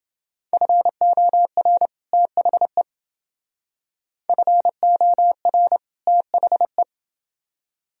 Morse code, heard as a keyed tone, about 29 wpm, 700 Hz. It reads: FOR THE FOR THE